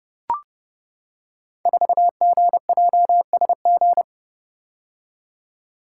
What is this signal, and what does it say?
Morse code, heard as a keyed tone, about 28 wpm, 700 Hz. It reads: 4GJSG
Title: Code Groups - 5 Letter:Numbers - 30wpm